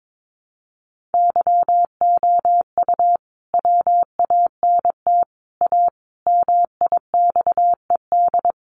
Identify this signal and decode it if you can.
Morse code, heard as a keyed tone, about 22 wpm, 700 Hz. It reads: YOU WANT A MIXED
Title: Sentences from Top 1500 Words 22wpm